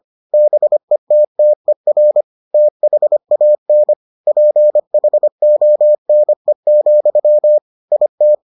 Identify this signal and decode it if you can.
Morse code, heard as a keyed tone, about 25 wpm, 600 Hz. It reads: BETTER THAN PHONE, IT